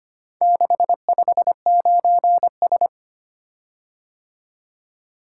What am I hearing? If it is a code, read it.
Morse code, heard as a keyed tone, about 25 wpm, 700 Hz. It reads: 659S
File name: practice / 25wpm / Four Letter-Number Combinations - 25wpm